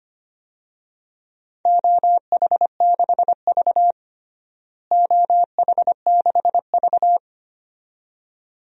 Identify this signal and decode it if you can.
Morse code, heard as a keyed tone, about 25 wpm, 700 Hz. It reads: OH6V OH6V
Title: DXCC-100 Call Signs - 4 Characters - Repeated 2x - 25wpm